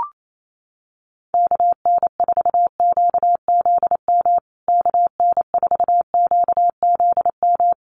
Morse code, heard as a keyed tone, about 28 wpm, 700 Hz. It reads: KN4QZM KN4QZM
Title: USA Call Signs Repeated 3x 28wpm